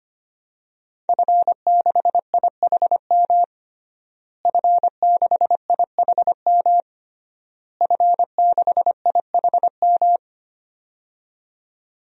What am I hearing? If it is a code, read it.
Morse code, heard as a keyed tone, about 25 wpm, 700 Hz. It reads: F6IHM F6IHM F6IHM
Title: DXCC-100 Call Signs - 5 Characters - Repeated 3x - 25wpm